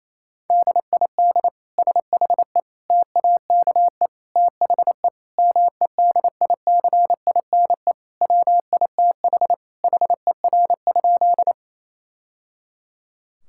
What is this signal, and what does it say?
Morse code, heard as a keyed tone, about 28 wpm, 700 Hz. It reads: DID SHE TAKE THE MEDICINE WITH HER?
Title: Sentences from Top 1000 Words - Review Part 6 - 28wpm